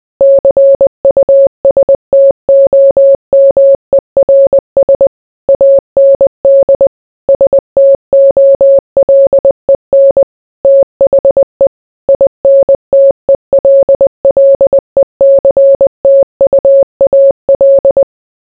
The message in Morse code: CUSTOMERS AND STOLEN THE INTELLECTUAL

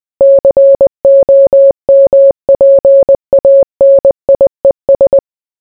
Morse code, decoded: COMPANIES

